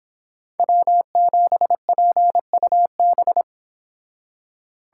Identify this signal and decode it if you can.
Morse code, heard as a keyed tone, about 31 wpm, 700 Hz. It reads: W7PUB